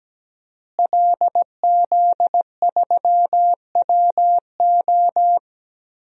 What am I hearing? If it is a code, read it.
Morse code, heard as a keyed tone, about 17 wpm, 700 Hz. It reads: LZ3WO
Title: DXCC-100 Call Signs - 5 Characters - Repeated 2x - 17wpm